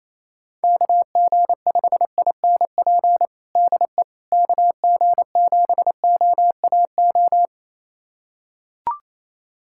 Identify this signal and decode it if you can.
Morse code, heard as a keyed tone, about 27 wpm, 700 Hz. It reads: KG5INP DE KG7OAO
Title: USA Callsign DE Callsign - 6 Characters - 28wpm